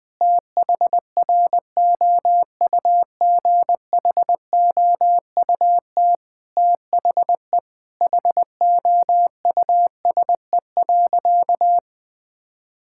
Morse code, heard as a keyed tone, about 20 wpm, 700 Hz. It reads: THROUGHOUT THE HOUSE.
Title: Sentences from Top 1000 Words - Review Part 5 - 20wpm